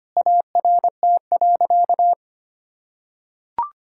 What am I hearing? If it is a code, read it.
Morse code, heard as a keyed tone, about 25 wpm, 700 Hz. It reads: ART.
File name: practice / 25wpm / Sentences with 4-Word Phrases - 25wpm